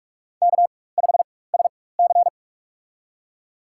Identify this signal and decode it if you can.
Morse code, heard as a keyed tone, about 44 wpm, spaced as for 21 wpm, 690 Hz. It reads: K5SC
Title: USA Call Signs - 4 Characters - 4x Character Spacing - 45wpm